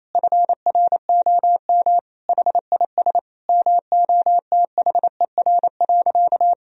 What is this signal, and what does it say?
Morse code, heard as a keyed tone, about 28 wpm, 700 Hz. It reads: FROM HIS MOTHER.
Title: Sentences from Top 1000 Words - Review Part 4 - 28wpm